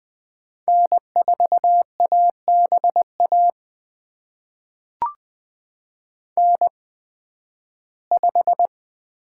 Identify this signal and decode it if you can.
Morse code, heard as a keyed tone, about 20 wpm, spaced as for 13 wpm, 700 Hz. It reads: N4ABA N5
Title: USA Call Signs - 8x Character Spacing - 20wpm